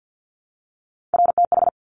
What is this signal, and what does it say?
Morse code, heard as a keyed tone, about 50 wpm, 700 Hz. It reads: RTH
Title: Top Three Letter Combinations in Words 50wpm